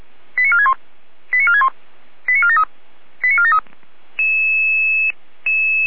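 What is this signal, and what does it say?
Five-tone selective call (ZVEI), heard as quick runs of five tones, tones of about 70 ms. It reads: 89461 89461 89473 89473
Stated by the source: ZVEI five-tone selective call received on an SDR.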